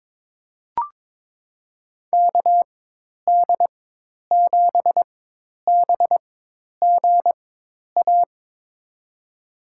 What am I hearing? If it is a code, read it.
Morse code, heard as a keyed tone, about 22 wpm, spaced as for 10 wpm, 700 Hz. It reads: KD7BGA
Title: USA Call Signs - 6 Characters - 4x Character Spacing - 22wpm